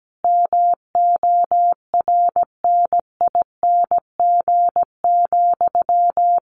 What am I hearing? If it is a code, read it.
Morse code, heard as a keyed tone, about 17 wpm, 700 Hz. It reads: MORNING,
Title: Sentences from Top 1400 Words 17wpm